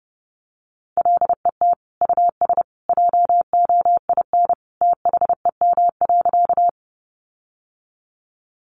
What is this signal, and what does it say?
Morse code, heard as a keyed tone, about 30 wpm, 700 Hz. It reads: LET US JOIN THEM.
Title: Sentences from Top 1400 Words 30wpm